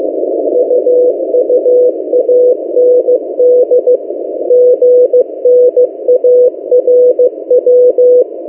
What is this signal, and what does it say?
Morse code, heard as a keyed tone, about 15 wpm, 500 Hz. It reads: UUAND GNARW
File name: M08a- -1355utc-8096khz